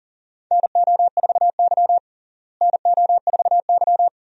Morse code, heard as a keyed tone, about 40 wpm, 700 Hz. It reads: NO4Y NO4Y
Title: USA Call Signs - 4 Characters - Repeated 2x - 40wpm